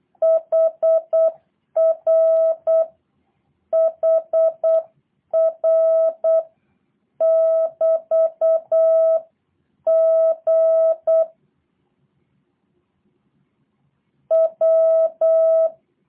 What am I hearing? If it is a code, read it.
Morse code, heard as a keyed tone, about 8 wpm, 650 Hz. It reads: HR HR <BT> G W
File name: jh8cbh / 2023cw / review1-4